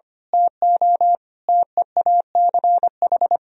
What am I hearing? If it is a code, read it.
Morse code, heard as a keyed tone, about 25 wpm, 700 Hz. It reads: TO TEACH
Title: Sentences from Top 1000 Words - Review Part 2 - 25wpm